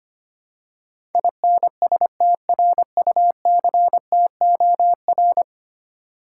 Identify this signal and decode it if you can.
Morse code, heard as a keyed tone, about 25 wpm, 700 Hz. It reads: INSTRUCTOR
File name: Ten Letter Words - 25wpm